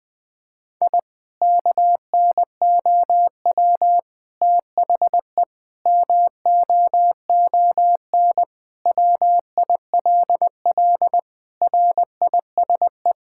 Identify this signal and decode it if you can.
Morse code, heard as a keyed tone, about 20 wpm, 700 Hz. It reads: I KNOW THE MOON WILL RISE
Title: Sentences from Top 1000 Words - Review Part 6 - 20wpm